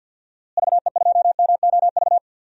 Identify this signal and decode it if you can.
Morse code, heard as a keyed tone, about 50 wpm, 700 Hz. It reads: VE2MOV
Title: DXCC-100 Call Signs - 6 Characters - Repeated 3x - 50wpm